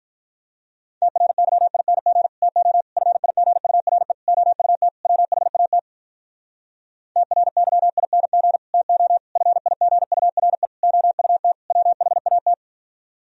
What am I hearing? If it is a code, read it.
Morse code, heard as a keyed tone, about 47 wpm, 700 Hz. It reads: TRYING TO FIGURE OUT WHAT TRYING TO FIGURE OUT WHAT